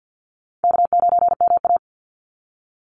Morse code, heard as a keyed tone, about 50 wpm, 700 Hz. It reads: K9MA
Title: Call Signs Repeated 2x - 50wpm